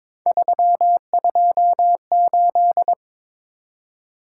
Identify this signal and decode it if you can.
Morse code, heard as a keyed tone, about 26 wpm, 700 Hz. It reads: 328
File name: Speed-Racing - Three Letter-Number Combinations - 22wpm